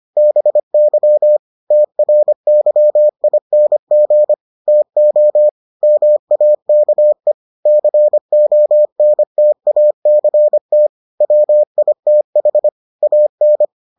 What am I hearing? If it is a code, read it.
Morse code, heard as a keyed tone, about 25 wpm, 600 Hz. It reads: BY TRYING TO MAKE CONTACT WITH AN